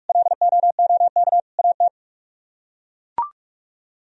Morse code, heard as a keyed tone, about 43 wpm, 700 Hz. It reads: LOOK AT